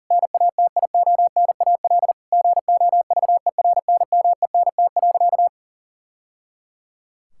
Morse code, heard as a keyed tone, about 37 wpm, 700 Hz. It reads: NATIONAL GOVERNMENT.
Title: Sentences from Top 1000 Words - Review Part 6 - 40wpm